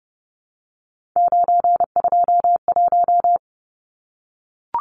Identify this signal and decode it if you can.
Morse code, heard as a keyed tone, about 30 wpm, 700 Hz. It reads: 921
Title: Numbers - 3-Digit 30wpm